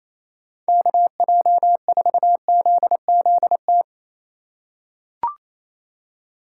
Morse code, heard as a keyed tone, about 27 wpm, 700 Hz. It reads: KJ4ZZT
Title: USA Call Signs - 6 Characters - 2x Character Spacing - 28wpm